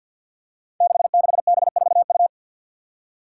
Morse code, heard as a keyed tone, about 50 wpm, 700 Hz. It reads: 66B4U
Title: Five Challenging Letter-Number Combinations 50wpm